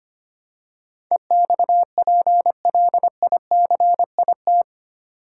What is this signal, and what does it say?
Morse code, heard as a keyed tone, about 25 wpm, 700 Hz. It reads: EXPLICIT
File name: Eight Letter Words - 25wpm